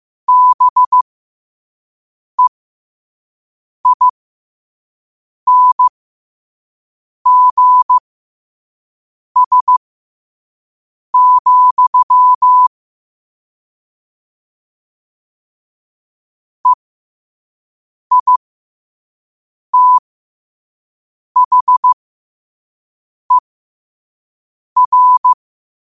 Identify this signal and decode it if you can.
Morse code, heard as a keyed tone, about 15 wpm, spaced as for 5 wpm, 1000 Hz. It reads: BEINGS, EITHER
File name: qotdfiles / 5wpm